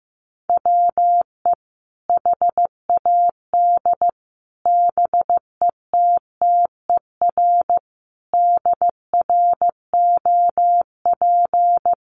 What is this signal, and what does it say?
Morse code, heard as a keyed tone, about 15 wpm, 700 Hz. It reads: WE HAD BETTER DROP
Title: Sentences from Top 1400 Words 15wpm